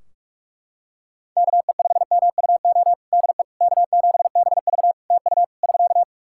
Morse code, heard as a keyed tone, about 41 wpm, 700 Hz. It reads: KE5MUO DE K7BV TU <SK>
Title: K1USN SST QSO Practice 45wpm